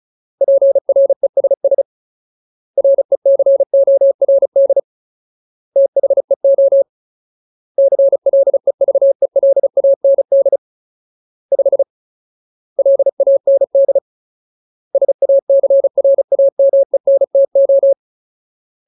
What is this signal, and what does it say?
Morse code, heard as a keyed tone, about 34 wpm, 550 Hz. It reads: PRESS RECORD THEO CLEVELAND 5 LAND SACRAMENTO
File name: Ultimate Word List / 35WPM Ultimate